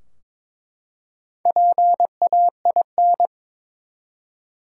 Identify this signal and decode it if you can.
Morse code, heard as a keyed tone, about 23 wpm, 700 Hz. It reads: PAIN
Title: Four Letter Words - 22wpm